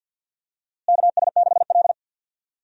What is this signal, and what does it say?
Morse code, heard as a keyed tone, about 48 wpm, 700 Hz. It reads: KS6L
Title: USA Call Signs - 4 Characters - Repeated 2x - 50wpm